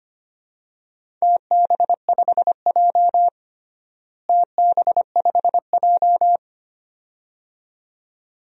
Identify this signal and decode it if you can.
Morse code, heard as a keyed tone, about 25 wpm, 700 Hz. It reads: TB5J TB5J
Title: DXCC-100 Call Signs - 4 Characters - Repeated 2x - 25wpm